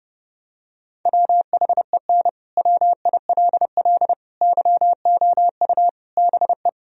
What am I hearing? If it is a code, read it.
Morse code, heard as a keyed tone, about 30 wpm, 700 Hz. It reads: WHEN WILL YOU BE
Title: Sentences from Top 1000 Words - Review Part 4 - 30wpm